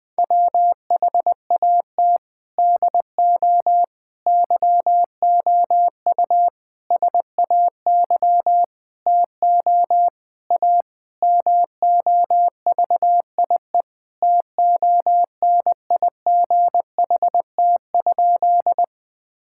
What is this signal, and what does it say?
Morse code, heard as a keyed tone, about 21 wpm, 700 Hz. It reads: WHAT DO YOU SAY TO A MOVIE TONIGHT?
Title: Sentences with 5-Word Phrases - 20wpm